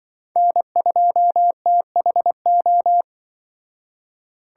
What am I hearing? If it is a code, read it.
Morse code, heard as a keyed tone, about 25 wpm, 700 Hz. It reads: N2THO
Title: Speed Racing - USA Call Signs 20wpm